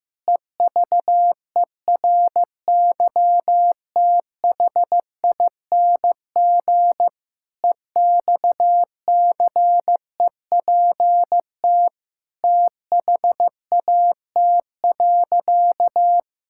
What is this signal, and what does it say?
Morse code, heard as a keyed tone, about 15 wpm, 700 Hz. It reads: EVERYTHING EXCEPT THAT.
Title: Sentences from Top 1600 Words 15wpm